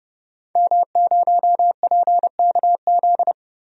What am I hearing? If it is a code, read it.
Morse code, heard as a keyed tone, about 30 wpm, 700 Hz. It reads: M0PKZ
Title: DXCC-100 Call Signs - 5 Characters - 30wpm